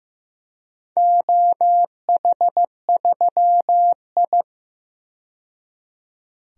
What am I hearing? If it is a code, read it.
Morse code, heard as a keyed tone, about 15 wpm, 700 Hz. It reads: OH3I
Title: DXCC-100 Call Signs - 4 Characters - 15wpm